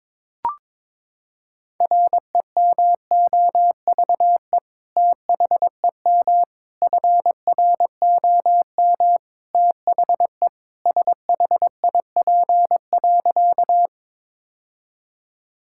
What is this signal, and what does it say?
Morse code, heard as a keyed tone, about 21 wpm, 700 Hz. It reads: REMOVE THEM FROM THE SHIP.
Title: Sentences from Top 1000 Words - Review Part 5 - 22wpm